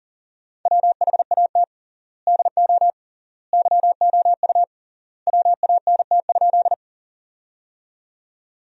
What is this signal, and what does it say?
Morse code, heard as a keyed tone, about 40 wpm, 700 Hz. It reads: WHAT DO YOU WANT?
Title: 3x Word-Spacing - Sentences from Top 200 Words - 40wpm